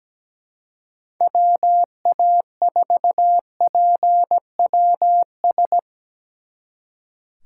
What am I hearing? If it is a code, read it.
Morse code, heard as a keyed tone, about 20 wpm, 700 Hz. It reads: WA4PWS
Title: Speed Racing - USA 2x3 Call Signs 17wpm